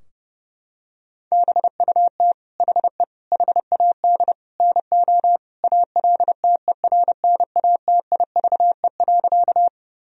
Morse code, heard as a keyed tone, about 32 wpm, 700 Hz. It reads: BUT HE HAD NO ALTERNATIVE.
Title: Sentences from Top 1000 Words - Review Part 4 - 30wpm